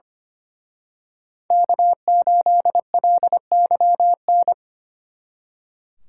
Morse code, heard as a keyed tone, about 29 wpm, 700 Hz. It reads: K8LYN
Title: Speed Racing - USA 1x3 Call Signs 25wpm